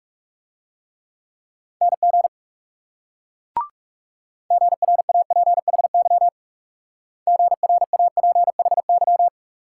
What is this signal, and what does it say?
Morse code, heard as a keyed tone, about 39 wpm, 700 Hz. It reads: NG GRAPHY GRAPHY